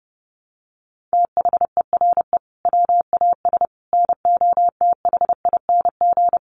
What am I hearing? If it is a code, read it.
Morse code, heard as a keyed tone, about 30 wpm, 700 Hz. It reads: THERE WAS NOTHING
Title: Sentences from Top 1300 Words 30wpm